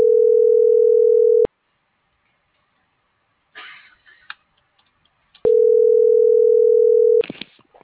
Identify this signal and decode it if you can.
Ringback tone (North America), 440 + 480 Hz.